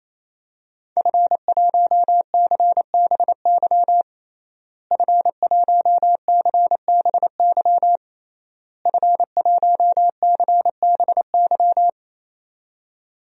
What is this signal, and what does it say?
Morse code, heard as a keyed tone, about 28 wpm, 700 Hz. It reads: F1CBY F1CBY F1CBY
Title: DXCC-100 Call Signs - 5 Characters - Repeated 3x - 28wpm